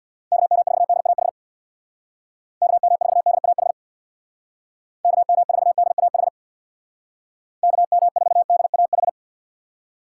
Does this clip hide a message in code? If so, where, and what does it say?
Yes: Morse code from 0.3 to 9.1 s, heard as a keyed tone, about 64 wpm, 700 Hz. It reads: KM4DAH KM4DAH KM4DAH KM4DAH